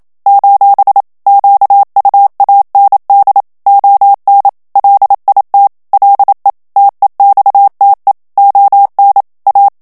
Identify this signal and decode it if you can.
Morse code, heard as a keyed tone, about 27 wpm, 800 Hz. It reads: : QUAND ON LIT LE TEXTE ON A
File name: FRANCAIS TEXTE CLAIR 1 VITESSE 26 MOTS MINUTES